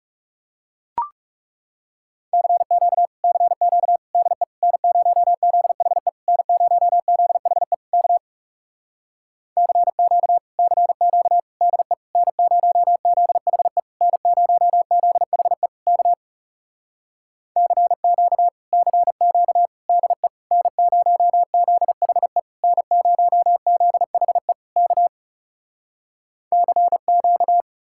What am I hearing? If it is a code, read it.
Morse code, heard as a keyed tone, about 40 wpm, 700 Hz. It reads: CQ CQ DE N0ZHE N0ZHE K CQ CQ DE N0ZHE N0ZHE K CQ CQ DE N0ZHE N0ZHE K CQ